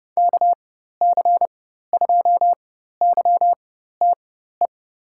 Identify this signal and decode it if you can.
Morse code, heard as a keyed tone, about 30 wpm, spaced as for 14 wpm, 700 Hz. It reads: KC2YTE
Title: USA Call Signs - 6 Characters - 4x Character Spacing - 30wpm